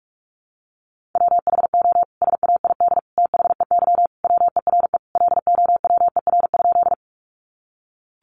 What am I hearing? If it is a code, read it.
Morse code, heard as a keyed tone, about 44 wpm, 700 Hz. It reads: WHO SAID THEY WERE LOWER?
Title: Sentences from Top 700 Words 45wpm